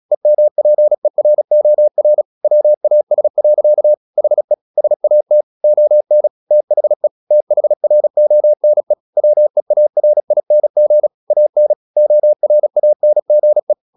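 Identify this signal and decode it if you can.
Morse code, heard as a keyed tone, about 36 wpm, 600 Hz. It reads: EMPEROR WAS. HE SAT ON THE THRONE WEARING AN ORANGE
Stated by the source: Morse practice at 36 wpm, text of Ken Follett's novel Never, from CW med Gnister.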